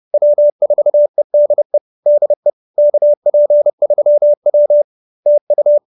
Morse code, heard as a keyed tone, about 30 wpm, 590 Hz. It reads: W4EDE DE KP3W TU
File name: QSO317 30